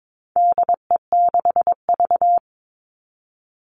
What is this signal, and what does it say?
Morse code, heard as a keyed tone, about 22 wpm, 700 Hz. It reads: DE6V
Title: All Cut Numbers - 4-Digit 22wpm